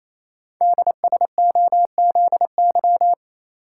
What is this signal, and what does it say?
Morse code, heard as a keyed tone, about 28 wpm, 700 Hz. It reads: DSOZY
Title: Code Groups - 5 Letters - 28wpm